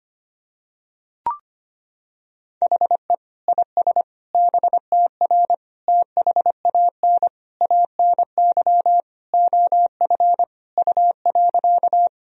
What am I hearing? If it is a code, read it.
Morse code, heard as a keyed tone, about 25 wpm, 700 Hz. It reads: HE IS BTR THAN ANY OF U.
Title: Sentences from Top 400 Words and Abbreviations 25wpm